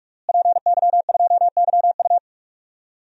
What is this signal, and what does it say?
Morse code, heard as a keyed tone, about 46 wpm, 700 Hz. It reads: PY2YU
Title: DXCC-100 Call Signs - 5 Characters - Repeated 3x - 45wpm